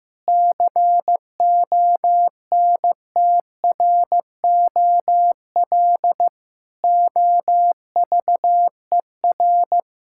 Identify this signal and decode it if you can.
Morse code, heard as a keyed tone, about 16 wpm, 700 Hz. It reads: CONTROL OVER
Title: Sentences from Top 1000 Words - Review Part 5 - 15wpm